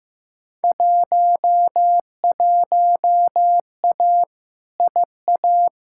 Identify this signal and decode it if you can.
Morse code, heard as a keyed tone, about 15 wpm, 700 Hz. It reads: 11A IA
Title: Field Day Exchanges Encore - 15wpm